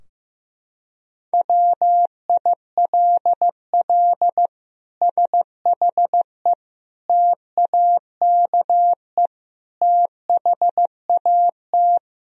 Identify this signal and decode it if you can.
Morse code, heard as a keyed tone, about 16 wpm, 700 Hz. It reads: WILL SHE TAKE THAT
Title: Sentences from Top 1000 Words - Review Part 4 - 15wpm